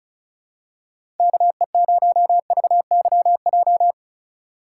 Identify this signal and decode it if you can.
Morse code, heard as a keyed tone, about 34 wpm, 700 Hz. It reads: KE0VYJ
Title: USA Call Signs - 6 Characters - 4x Character Spacing - 35wpm